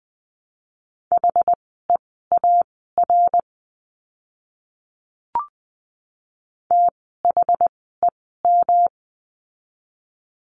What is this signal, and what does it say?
Morse code, heard as a keyed tone, about 20 wpm, spaced as for 14 wpm, 700 Hz. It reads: HEAR THEM
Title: 206 Lesson 33f - Letter j - Words 20wpm